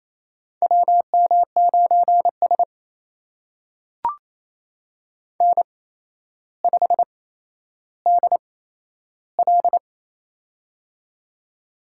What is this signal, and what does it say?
Morse code, heard as a keyed tone, about 28 wpm, spaced as for 12 wpm, 700 Hz. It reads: WM9S N5DL
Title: USA Call Signs - 4 Characters - 8x Character Spacing - 28wpm